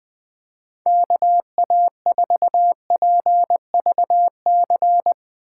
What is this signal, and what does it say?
Morse code, heard as a keyed tone, about 21 wpm, 700 Hz. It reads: KA4PVC
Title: USA Call Signs - 6 Characters - 6x Character Spacing - 20wpm